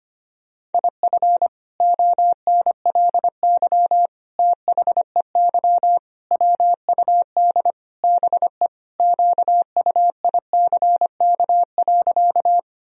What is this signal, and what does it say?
Morse code, heard as a keyed tone, about 25 wpm, 700 Hz. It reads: IF ONLY THEY WUD BE QUICK.
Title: Sentences from Top 500 Words and Abbreviations 25wpm